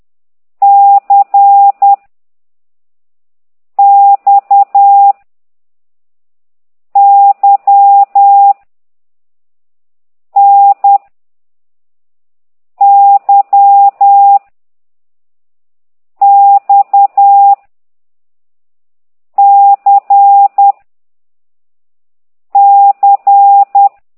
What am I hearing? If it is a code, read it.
Morse code, heard as a keyed tone, about 10 wpm, 800 Hz. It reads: C X Y N Y X C C
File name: ct2idw / exerc4